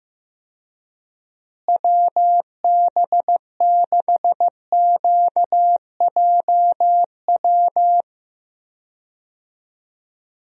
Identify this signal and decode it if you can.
Morse code, heard as a keyed tone, about 15 wpm, 700 Hz. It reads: WB6QJW
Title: USA Call Signs - 6 Characters - Repeated 3x - 15wpm